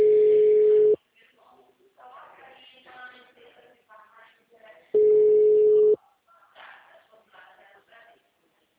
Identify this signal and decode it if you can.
Ringback tone (Europe, 425 Hz), with speech.
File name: out- -2010- - - .19183